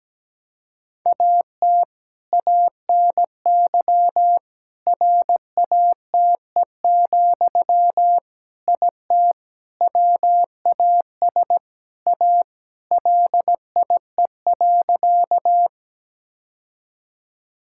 Morse code, heard as a keyed tone, about 17 wpm, 700 Hz. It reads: AT ANY RATE, IT WAS A LIE.
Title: Sentences from Top 1600 Words 17wpm